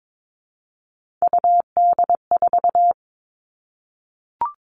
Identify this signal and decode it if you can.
Morse code, heard as a keyed tone, about 22 wpm, 700 Hz. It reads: UD4
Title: All Cut Numbers - 3-Digit 22wpm